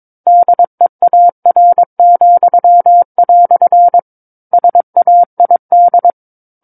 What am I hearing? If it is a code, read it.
Morse code, heard as a keyed tone, about 22 wpm, 700 Hz. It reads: DEAR," SAID
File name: 06-Chapter-06 22 WPM